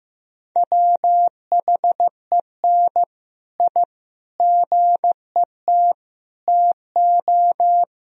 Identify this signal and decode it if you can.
Morse code, heard as a keyed tone, about 15 wpm, 700 Hz. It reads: WHEN I GET TO
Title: Sentences from Top 1000 Words - Review Part 1 - 15wpm